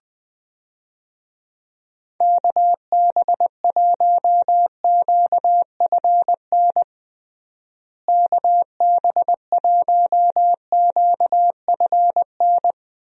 Morse code, heard as a keyed tone, about 21 wpm, 700 Hz. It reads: KB1QFN KB1QFN
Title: USA Call Signs - 6 Characters - Repeated 2x - 20wpm